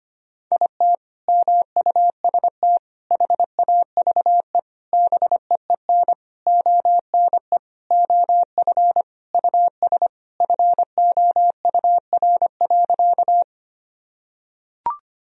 Morse code, heard as a keyed tone, about 25 wpm, 700 Hz. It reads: IT MUST HAVE BEEN ONE OF US FOUR.
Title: Sentences from Top 1000 Words - Review Part 1 - 25wpm